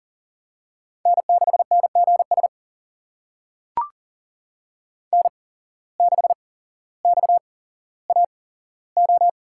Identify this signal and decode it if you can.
Morse code, heard as a keyed tone, about 40 wpm, spaced as for 17 wpm, 700 Hz. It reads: N6NGS N6XAO
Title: USA Call Signs - 5 Characters - 8x Character Spacing - 40wpm